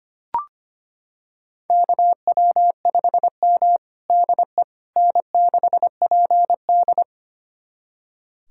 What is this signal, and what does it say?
Morse code, heard as a keyed tone, about 25 wpm, 700 Hz. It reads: KW5M DE N6PD
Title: USA Callsign DE Callsign - 4 Characters - 25wpm